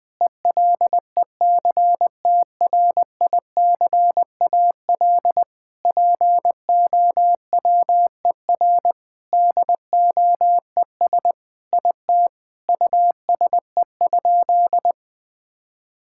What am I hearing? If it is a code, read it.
Morse code, heard as a keyed tone, about 20 wpm, 700 Hz. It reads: ELECTRICAL POWER DOES IT USE?
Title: Sentences from Top 1000 Words - Review Part 6 - 20wpm